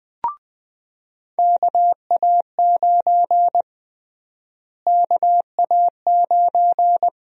Morse code, heard as a keyed tone, about 21 wpm, 700 Hz. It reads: KA9 KA9
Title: USA Call Sign Prefixes - Repeated 2x - 20wpm